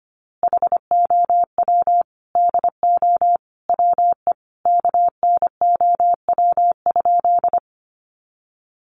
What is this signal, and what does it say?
Morse code, heard as a keyed tone, about 25 wpm, 700 Hz. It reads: HOW DO WE KNOW?
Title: A Sound Of Thunder 25wpm